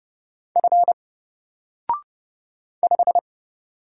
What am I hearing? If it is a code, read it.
Morse code, heard as a keyed tone, about 30 wpm, 700 Hz. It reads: F 5
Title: Single Letter-Number 30wpm